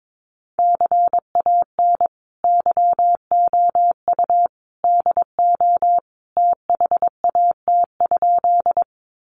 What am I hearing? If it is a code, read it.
Morse code, heard as a keyed tone, about 22 wpm, 700 Hz. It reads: CAN YOU DO THAT?
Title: Sentences from Top 100 Words 22wpm